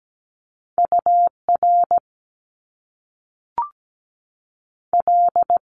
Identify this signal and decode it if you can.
Morse code, heard as a keyed tone, about 17 wpm, 700 Hz. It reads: UR L
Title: Top Two Letter Combinations in Words 17wpm